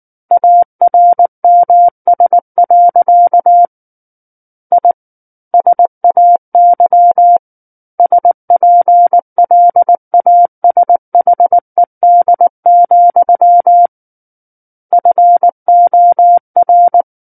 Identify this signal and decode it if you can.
Morse code, heard as a keyed tone, about 19 wpm, 700 Hz. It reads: ARMS. I SAY SPLASHED, FOR
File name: Chapter 10 - 19 WPM